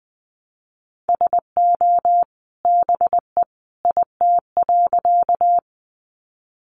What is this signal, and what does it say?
Morse code, heard as a keyed tone, about 20 wpm, 700 Hz. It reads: SO BE IT.
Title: Sentences from Top 100 Words 20wpm